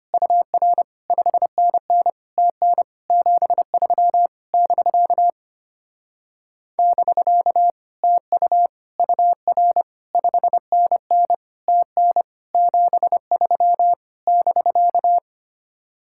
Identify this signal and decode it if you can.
Morse code, heard as a keyed tone, about 30 wpm, 700 Hz. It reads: UR 5NN TN 73 <BK> <BK> TU UR 5NN TN 73 <BK>
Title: Speed-Racing Common QSO Phrases - 25wpm